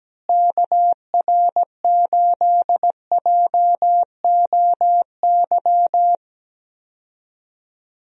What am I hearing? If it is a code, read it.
Morse code, heard as a keyed tone, about 17 wpm, 700 Hz. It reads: KR8JOY
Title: USA Call Signs - 6 Characters - Repeated 3x - 17wpm